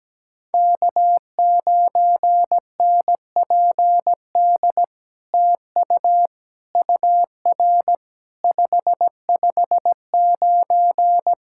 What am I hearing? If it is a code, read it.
Morse code, heard as a keyed tone, about 17 wpm, 700 Hz. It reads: K9NPD TU UR 559
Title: SOTA QSO Practice 17wpm